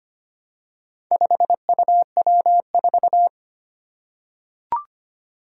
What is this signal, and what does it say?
Morse code, heard as a keyed tone, about 25 wpm, 700 Hz. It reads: 5UW4
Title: Four Challenging Letter-Number Combinations 25wpm